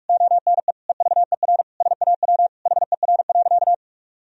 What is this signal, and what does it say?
Morse code, heard as a keyed tone, about 46 wpm, 700 Hz. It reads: ONE EVER SAW HER.
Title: Sentences from Top 1000 Words - Review Part 2 - 45wpm